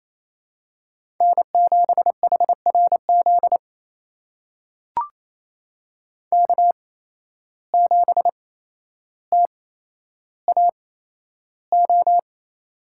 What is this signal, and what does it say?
Morse code, heard as a keyed tone, about 28 wpm, spaced as for 12 wpm, 700 Hz. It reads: N7HRZ K7TAO
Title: USA Call Signs - 5 Characters - 8x Character Spacing - 28wpm